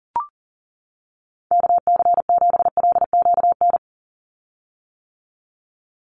Morse code, heard as a keyed tone, about 40 wpm, 700 Hz. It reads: KC7LQN
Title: USA Call Signs 40wpm